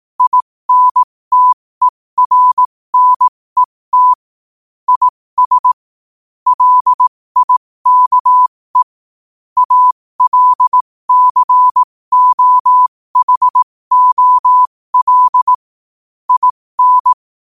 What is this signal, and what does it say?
Morse code, heard as a keyed tone, about 18 wpm, spaced as for 15 wpm, 1000 Hz. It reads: INTERNET IS LIKE ALCOHOL IN
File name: qotdfiles / 15wpm